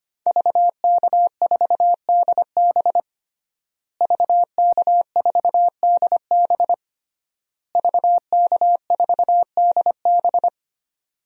Morse code, heard as a keyed tone, about 25 wpm, 700 Hz. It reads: VK4DB VK4DB VK4DB
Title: DXCC-100 Call Signs - 5 Characters - Repeated 3x - 25wpm